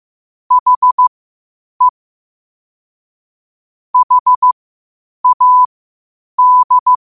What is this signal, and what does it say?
Morse code, heard as a keyed tone, about 15 wpm, spaced as for 8 wpm, 1000 Hz. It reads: HE HAD
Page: Quotes for Sat, 16 Aug 2025 in Morse Code at 8 words per minute.